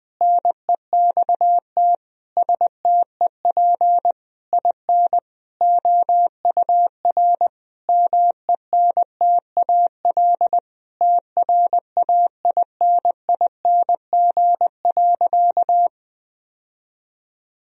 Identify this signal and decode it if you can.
Morse code, heard as a keyed tone, about 21 wpm, 700 Hz. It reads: NEXT STEP IN OUR MENTAL TRAINING.
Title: Sentences from Top 1000 Words - Review Part 4 - 20wpm